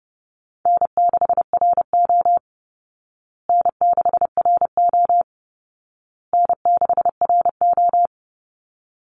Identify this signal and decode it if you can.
Morse code, heard as a keyed tone, about 30 wpm, 700 Hz. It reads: N6RO N6RO N6RO
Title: Easy Call Signs Repeated 3x - 30wpm